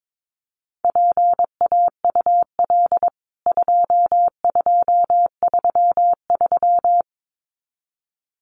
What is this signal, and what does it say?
Morse code, heard as a keyed tone, about 22 wpm, 700 Hz. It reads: PAUL 2233
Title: CWT Exchange 22wpm